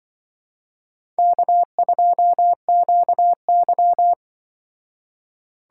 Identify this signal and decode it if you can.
Morse code, heard as a keyed tone, about 24 wpm, 700 Hz. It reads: K2QY
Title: Speed Racing - USA Call Signs 20wpm